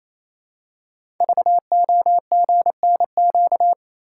Morse code, heard as a keyed tone, about 27 wpm, 700 Hz. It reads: VOGNQ
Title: Code Groups - 5 Letters - 28wpm